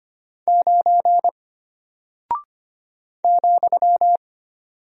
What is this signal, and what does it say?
Morse code, heard as a keyed tone, about 25 wpm, 700 Hz. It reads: 9 ,
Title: Single Character (Letter:Number:Punctuation:Prosign) Alt 25wpm